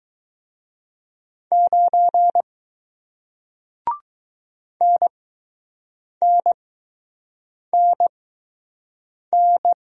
Morse code, heard as a keyed tone, about 20 wpm, spaced as for 6 wpm, 700 Hz. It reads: 9 NNNN